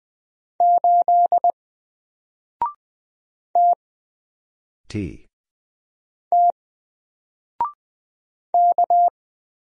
Morse code, heard as a keyed tone, about 20 wpm, 700 Hz. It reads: 8 T T K
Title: Single Character (Letter:Number:Punctuation:Prosign) 20wpm